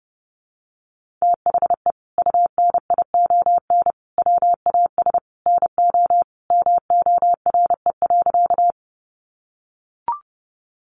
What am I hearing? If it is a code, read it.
Morse code, heard as a keyed tone, about 30 wpm, 700 Hz. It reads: THE UNION WAS NO MORE.
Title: Sentences from Top 1400 Words 30wpm